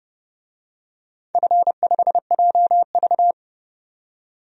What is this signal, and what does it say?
Morse code, heard as a keyed tone, about 29 wpm, 700 Hz. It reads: F5JV
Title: DXCC-100 Call Signs - 4 Characters - 30wpm